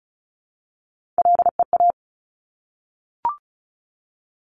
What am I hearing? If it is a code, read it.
Morse code, heard as a keyed tone, about 36 wpm, 700 Hz. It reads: LEA
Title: Top Three Letter Combinations in Words 35wpm